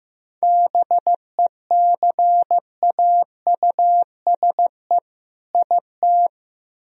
Morse code, heard as a keyed tone, about 15 wpm, 700 Hz. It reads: BECAUSE IT
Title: Speed-Racing - Sets of 2 Words 15wpm